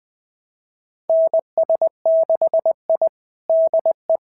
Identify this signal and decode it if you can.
Morse code, heard as a keyed tone, about 20 wpm, 650 Hz. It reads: NS6I DE
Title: SOTA QSO Practice 20wpm